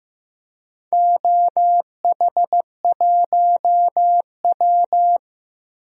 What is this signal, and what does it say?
Morse code, heard as a keyed tone, about 15 wpm, 700 Hz. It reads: OH1W
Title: DXCC-100 Call Signs - 4 Characters - 15wpm